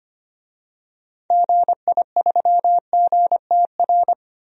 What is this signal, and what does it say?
Morse code, heard as a keyed tone, about 25 wpm, 700 Hz. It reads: GI3GTR
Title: DXCC-100 Call Signs - 6 Characters - Repeated 2x - 25wpm